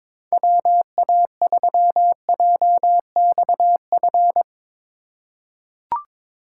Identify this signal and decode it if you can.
Morse code, heard as a keyed tone, about 22 wpm, 700 Hz. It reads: WA3JXF
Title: USA Call Signs - 6 Characters - 2x Character Spacing - 22wpm